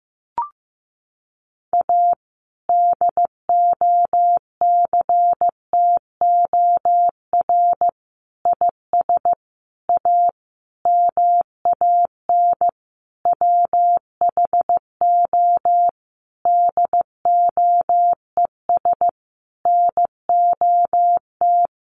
Morse code, heard as a keyed tone, about 15 wpm, 700 Hz. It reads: A DOCTOR IS A MAN WHO DOES NOT
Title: Sentences from Top 1500 Words 15wpm